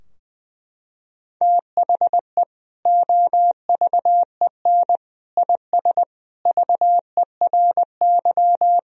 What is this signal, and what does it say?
Morse code, heard as a keyed tone, about 20 wpm, 700 Hz. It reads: THE OVEN IS VERY
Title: Sentences from Top 1000 Words - Review Part 4 - 20wpm